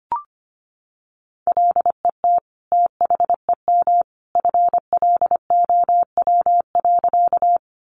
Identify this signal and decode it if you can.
Morse code, heard as a keyed tone, about 25 wpm, 700 Hz. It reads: LET THEM FLOW.
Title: Sentences from Top 1200 Words 25wpm